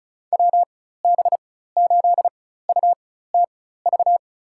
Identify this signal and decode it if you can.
Morse code, heard as a keyed tone, about 34 wpm, spaced as for 16 wpm, 690 Hz. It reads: WB8UTV
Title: USA Call Signs - 4x Character Spacing - 35wpm